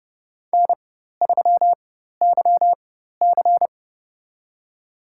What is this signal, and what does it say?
Morse code, heard as a keyed tone, about 30 wpm, spaced as for 14 wpm, 700 Hz. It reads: N3YC
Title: USA Call Signs - 4 Characters - 4x Character Spacing - 30wpm